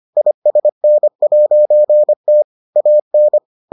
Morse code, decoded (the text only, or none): ISN'T AN